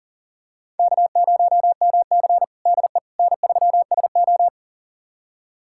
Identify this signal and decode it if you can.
Morse code, heard as a keyed tone, about 40 wpm, 700 Hz. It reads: K0MC DE N3SO
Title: USA Callsign DE Callsign - 4 Characters - 40wpm